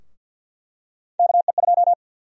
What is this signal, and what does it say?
Morse code, heard as a keyed tone, about 48 wpm, 700 Hz. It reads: KE2
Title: USA Call Sign Prefixes - 50wpm